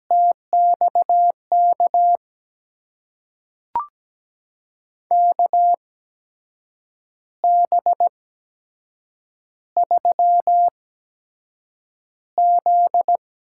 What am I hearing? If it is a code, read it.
Morse code, heard as a keyed tone, about 17 wpm, spaced as for 7 wpm, 700 Hz. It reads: TXK KB3Z
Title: USA Call Signs - 6 Characters - 8x Character Spacing - 17wpm